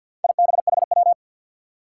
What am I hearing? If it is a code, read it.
Morse code, heard as a keyed tone, about 53 wpm, 700 Hz. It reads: IBHW
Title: Four Challenging Letter-Number Combinations 50wpm